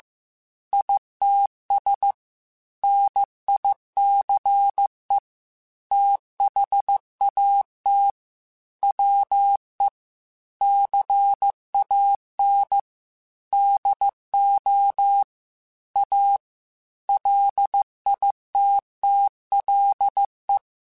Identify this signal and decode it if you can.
Morse code, heard as a keyed tone, about 15 wpm, 800 Hz. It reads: ITS NICE THAT WE CAN DO A LITTLE
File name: QSOs / QSOs-mp3s / WD0BGP 2@13